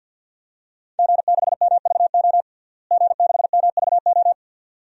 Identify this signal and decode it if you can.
Morse code, heard as a keyed tone, about 52 wpm, 700 Hz. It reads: G6MVO G6MVO
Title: DXCC-100 Call Signs - 5 Characters - Repeated 2x - 50wpm